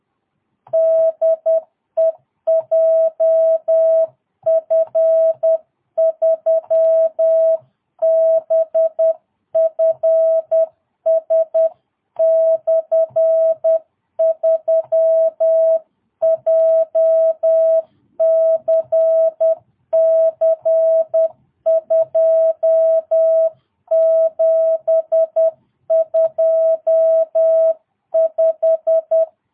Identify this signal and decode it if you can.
Morse code, heard as a keyed tone, about 10 wpm, 650 Hz. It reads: DE JF3BFS/3 JCC 2725
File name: jh8cbh / 2022cw